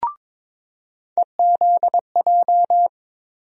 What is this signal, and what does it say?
Morse code, heard as a keyed tone, about 22 wpm, 700 Hz. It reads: EZJ
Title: USA Call Sign Postfixes - 3 Letters 22wpm